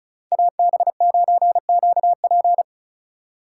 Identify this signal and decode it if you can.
Morse code, heard as a keyed tone, about 36 wpm, 700 Hz. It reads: AB9QP
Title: USA Call Signs - 5 Characters - 2x Character Spacing - 35wpm